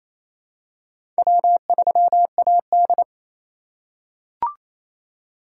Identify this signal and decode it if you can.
Morse code, heard as a keyed tone, about 28 wpm, 700 Hz. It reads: W3AD
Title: USA Call Signs - 4 Characters - 3x Character Spacing - 28wpm